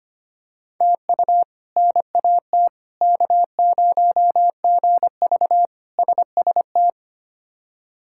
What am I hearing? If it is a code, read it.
Morse code, heard as a keyed tone, about 25 wpm, 700 Hz. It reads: TU NAT K0GV SST
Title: K1USN SST QSO Practice - 25wpm